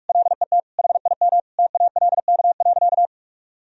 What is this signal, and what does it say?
Morse code, heard as a keyed tone, about 46 wpm, 700 Hz. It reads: LET HIM TALK.
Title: Sentences from Top 1000 Words - Review Part 4 - 45wpm